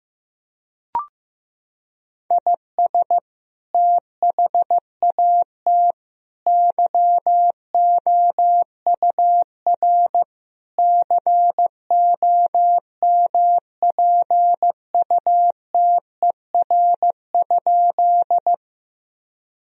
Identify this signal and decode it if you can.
Morse code, heard as a keyed tone, about 15 wpm, 700 Hz. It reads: IS THAT YOUR COMPUTER?
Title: Sentences from Top 1000 Words - Review Part 3 - 15wpm